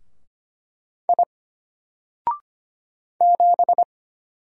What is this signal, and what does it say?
Morse code, heard as a keyed tone, about 25 wpm, 700 Hz. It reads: I 7
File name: Single-Letter-Number-25wpm